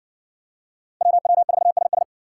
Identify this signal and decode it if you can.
Morse code, heard as a keyed tone, about 58 wpm, 700 Hz. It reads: WW4SS